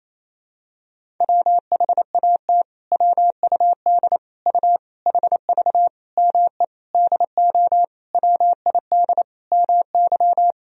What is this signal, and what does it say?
Morse code, heard as a keyed tone, about 28 wpm, 700 Hz. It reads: WHAT WUD U HV ME DO WID MY
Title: Sentences from Top 300 Words and Abbreviations 28wpm